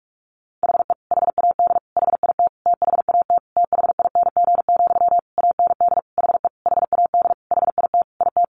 Morse code, heard as a keyed tone, about 44 wpm, 700 Hz. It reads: HE HAD HIT THAT THING, AND HE HAD HIT IT
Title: Sentences from Top 1200 Words 45wpm